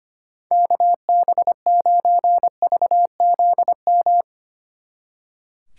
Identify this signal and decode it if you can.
Morse code, heard as a keyed tone, about 29 wpm, 700 Hz. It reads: KB9VZM
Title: Speed Racing - USA 2x3 Call Signs 25wpm